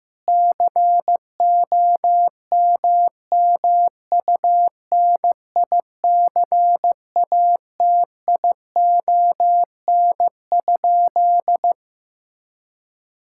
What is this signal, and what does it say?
Morse code, heard as a keyed tone, about 14 wpm, 700 Hz. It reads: COMMUNICATION?
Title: Sentences from Top 1000 Words - Review Part 6 - 15wpm